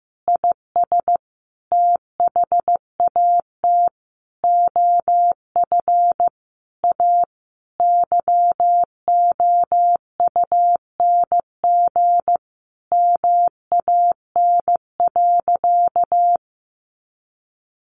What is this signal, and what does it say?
Morse code, heard as a keyed tone, about 14 wpm, 700 Hz. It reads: IS THAT OF A YOUNG MAN.
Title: Sentences from Top 500 Words 15wpm